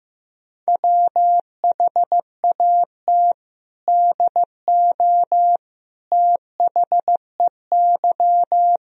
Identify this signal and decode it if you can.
Morse code, heard as a keyed tone, about 15 wpm, 700 Hz. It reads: WHAT DO THEY
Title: Sentences from Top 100 Words and Abbreviations 15wpm